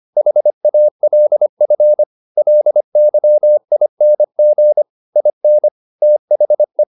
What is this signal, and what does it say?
Morse code, heard as a keyed tone, about 25 wpm, 600 Hz. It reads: HALF LYING IN THE